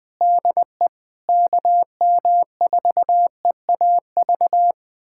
Morse code, heard as a keyed tone, about 20 wpm, 700 Hz. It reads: DE KM4EAV
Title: USA Callsign DE Callsign 20wpm